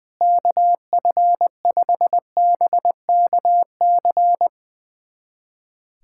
Morse code, heard as a keyed tone, about 21 wpm, 700 Hz. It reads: KF5BKC
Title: USA Callsign DE Callsign - 6 Characters - 20wpm